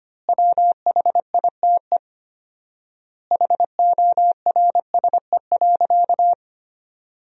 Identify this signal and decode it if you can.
Morse code, heard as a keyed tone, about 25 wpm, 700 Hz. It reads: WHITE HORSE.
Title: 4x Word-Spacing - Sentences from Top 300 Words - 25wpm